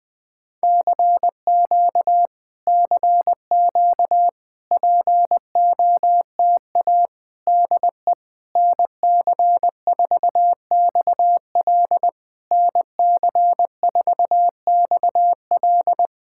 Morse code, heard as a keyed tone, about 20 wpm, 690 Hz. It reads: CQ CQ POTA DE NC4XL NC4XL
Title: POTA QSO Practice - 20wpm